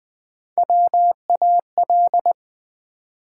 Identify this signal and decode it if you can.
Morse code, heard as a keyed tone, about 20 wpm, 700 Hz. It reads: WAL
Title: Build A Word - 1 Letter At A Time - 3 to 5-Letter Words - 20wpm